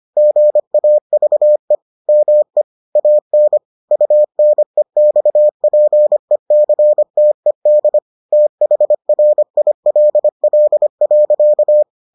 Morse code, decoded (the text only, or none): GAVE ME AN UNEXPECTED THRILL.